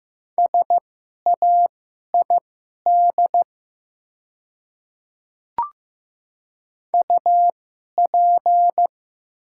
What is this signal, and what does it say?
Morse code, heard as a keyed tone, about 15 wpm, spaced as for 11 wpm, 700 Hz. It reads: SAID UP
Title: 2x Character Spacing - Top 200 Words - 15wpm